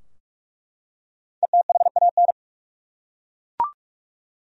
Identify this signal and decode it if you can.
Morse code, heard as a keyed tone, about 42 wpm, 700 Hz. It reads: ETHAN
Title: USA Common Names - 45wpm